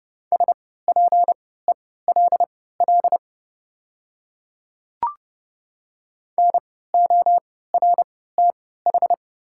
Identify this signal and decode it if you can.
Morse code, heard as a keyed tone, about 30 wpm, spaced as for 17 wpm, 700 Hz. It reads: SPELL NORTH
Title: 3x Character Spacing - Top 300 Words - 30wpm